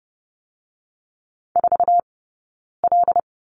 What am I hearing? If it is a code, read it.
Morse code, heard as a keyed tone, about 30 wpm, 700 Hz. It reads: 4 L
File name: 092 Lesson 16d - Letter c - Two Characters 30wpm